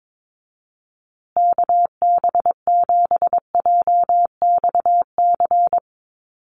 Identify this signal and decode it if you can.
Morse code, heard as a keyed tone, about 22 wpm, 700 Hz. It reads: KB7JXC
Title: USA Call Signs Repeated 3x 22wpm